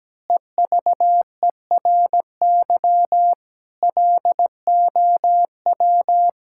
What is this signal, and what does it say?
Morse code, heard as a keyed tone, about 17 wpm, 700 Hz. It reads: EVERY LOW
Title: Pangram Challenge v2 - 17wpm